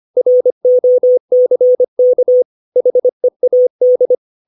Morse code, heard as a keyed tone, about 25 wpm, 500 Hz. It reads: ROCK HEAD
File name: 632 words / 632 Words 25WPM